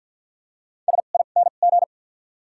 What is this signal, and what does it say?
Morse code, heard as a keyed tone, about 53 wpm, 700 Hz. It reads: S I N G
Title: 2x Character Spacing - Top 500 Words - 50wpm